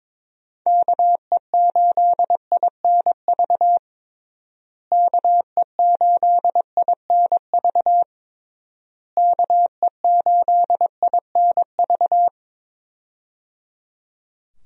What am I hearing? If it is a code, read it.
Morse code, heard as a keyed tone, about 22 wpm, 700 Hz. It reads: KE8INV KE8INV KE8INV
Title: USA Call Signs - 6 Characters - Repeated 3x - 22wpm